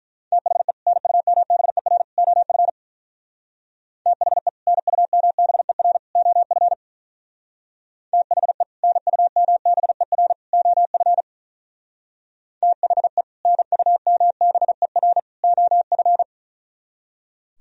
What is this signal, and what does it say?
Morse code, heard as a keyed tone, about 42 wpm, 700 Hz. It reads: THE NUMBER OF THE NUMBER OF THE NUMBER OF THE NUMBER OF
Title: Speed-Racing - Sets of 3 Words 35wpm